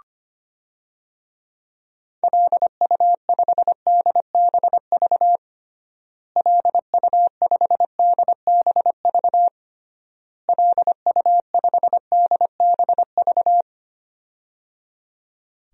Morse code, heard as a keyed tone, about 25 wpm, 700 Hz. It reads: LU5DBV LU5DBV LU5DBV
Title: DXCC-100 Call Signs - 6 Characters - Repeated 3x - 25wpm